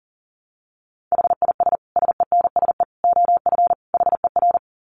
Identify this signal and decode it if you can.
Morse code, heard as a keyed tone, about 40 wpm, 700 Hz. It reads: HIS SENSE OF HER
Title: Dit Challenge 40wpm